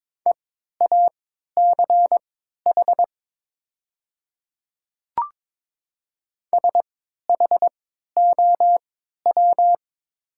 Morse code, heard as a keyed tone, about 22 wpm, spaced as for 12 wpm, 700 Hz. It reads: EACH SHOW
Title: 3x Character Spacing - Top 200 Words - 22wpm